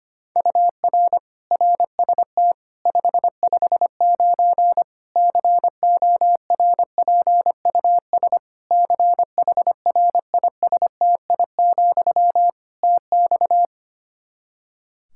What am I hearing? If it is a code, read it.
Morse code, heard as a keyed tone, about 25 wpm, 700 Hz. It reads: UR RST 559 CORPUS CHRISTI, TX
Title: Practice QSO - Exchange 1 - Beginning - 25wpm